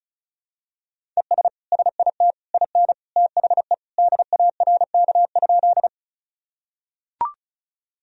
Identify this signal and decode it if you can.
Morse code, heard as a keyed tone, about 34 wpm, 700 Hz. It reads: ES SIT IN THE DARK?
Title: Sentences from Top 500 Words and Abbreviations 35wpm